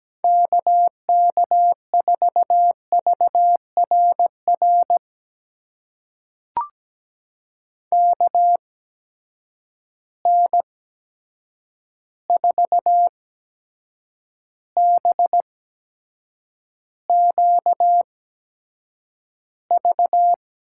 Morse code, heard as a keyed tone, about 17 wpm, spaced as for 7 wpm, 700 Hz. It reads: KK4VRR KN4BQV
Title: USA Call Signs - 6 Characters - 8x Character Spacing - 17wpm